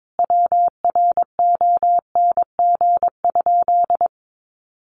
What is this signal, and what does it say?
Morse code, heard as a keyed tone, about 22 wpm, 700 Hz. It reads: WRONG?
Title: Sentences from Top 900 Words 22wpm